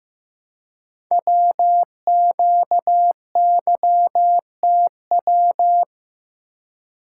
Morse code, heard as a keyed tone, about 15 wpm, 700 Hz. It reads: WQYTW
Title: Code Groups - 5 Letters - 15wpm